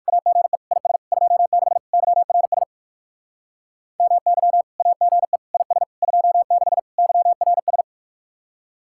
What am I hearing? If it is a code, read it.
Morse code, heard as a keyed tone, about 59 wpm, 700 Hz. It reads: AGE IS 26 YRS MY AGE IS 26 YRS